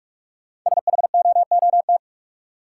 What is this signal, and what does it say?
Morse code, heard as a keyed tone, about 45 wpm, 700 Hz. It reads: SHOOT
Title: Sentences from Top 1200 Words - New Word Drill - 45wpm